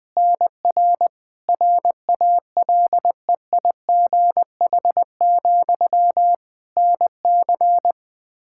Morse code, heard as a keyed tone, about 20 wpm, 700 Hz. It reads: NR RALEIGH, NC
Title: Practice QSO - Exchange 1 - Beginning - 20wpm